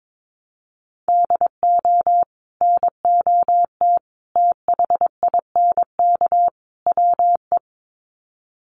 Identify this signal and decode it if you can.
Morse code, heard as a keyed tone, about 22 wpm, 700 Hz. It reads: DO NOT THINK WE
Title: Sets of 4 Words 22wpm